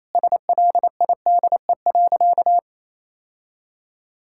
Morse code, heard as a keyed tone, about 28 wpm, 700 Hz. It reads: SLIDE.
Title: Sentences from Top 1600 Words 28wpm